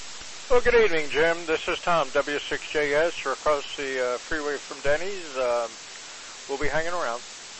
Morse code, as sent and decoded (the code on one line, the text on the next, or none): none